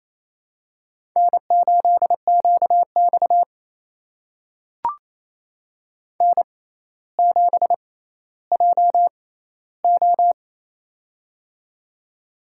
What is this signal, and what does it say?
Morse code, heard as a keyed tone, about 28 wpm, spaced as for 14 wpm, 700 Hz. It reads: N8QX N7JO
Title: USA Call Signs - 4 Characters - 6x Character Spacing - 28wpm